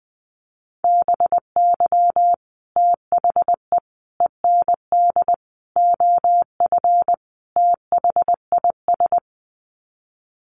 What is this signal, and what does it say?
Morse code, heard as a keyed tone, about 20 wpm, 700 Hz. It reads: BY THE END OF THIS
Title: Sets of 5 Words Encore 20wpm